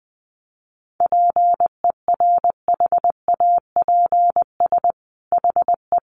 Morse code, heard as a keyed tone, about 20 wpm, 700 Hz. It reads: PERHAPS HE
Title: Sentences from Top 500 Words 20wpm